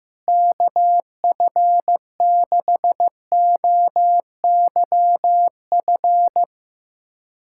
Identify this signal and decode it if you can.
Morse code, heard as a keyed tone, about 15 wpm, 700 Hz. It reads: KF6OYF
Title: USA Callsign DE Callsign - 6 Characters - 15wpm